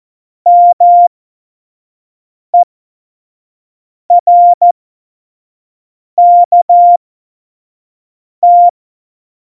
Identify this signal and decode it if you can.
Morse code, heard as a keyed tone, about 14 wpm, spaced as for 5 wpm, 700 Hz. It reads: MERKT
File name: Morsekurs / Witze / Witz 19 Mai 2020, 25 F70